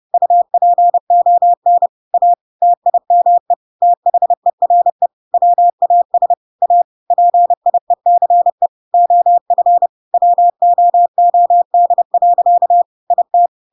Morse code, heard as a keyed tone, about 30 wpm, 700 Hz. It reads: UPON A TIME THERE WAS A PIECE OF WOOD. IT
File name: Pinocchio / Pinocchio ch1 30 30 ik1hge